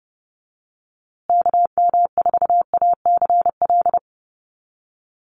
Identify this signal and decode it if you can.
Morse code, heard as a keyed tone, about 30 wpm, 700 Hz. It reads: KM4ACL
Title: USA Call Signs Repeated 3x 30wpm